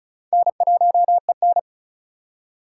Morse code, heard as a keyed tone, about 34 wpm, 700 Hz. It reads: N1EN
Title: K1USN SST QSO Practice - 35wpm